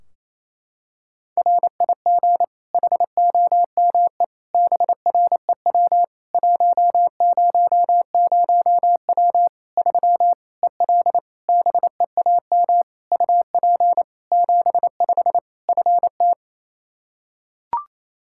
Morse code, heard as a keyed tone, about 28 wpm, 700 Hz. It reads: RIG HOME BREW 100W 3 EL BEAM UP 75 FT
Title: K1USN SST QSO Practice 28wpm